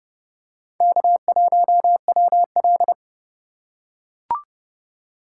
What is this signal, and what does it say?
Morse code, heard as a keyed tone, about 28 wpm, 700 Hz. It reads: K1WL